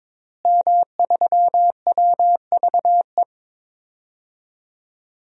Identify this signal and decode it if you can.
Morse code, heard as a keyed tone, about 21 wpm, 700 Hz. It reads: M3WVE
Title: DXCC-100 Call Signs - 5 Characters - Repeated 3x - 22wpm